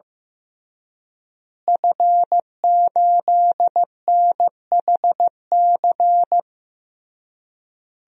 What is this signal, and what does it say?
Morse code, heard as a keyed tone, about 15 wpm, 700 Hz. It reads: F8NHC
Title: DXCC-100 Call Signs - 5 Characters - Repeated 3x - 15wpm